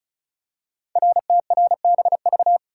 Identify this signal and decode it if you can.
Morse code, heard as a keyed tone, about 36 wpm, 700 Hz. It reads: RTRBV
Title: Code Groups - 5 Letters - 35wpm